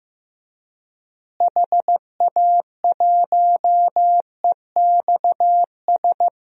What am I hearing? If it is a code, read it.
Morse code, heard as a keyed tone, about 16 wpm, 700 Hz. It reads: HA1EXS
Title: DXCC-100 Call Signs - 6 Characters - Repeated 3x - 15wpm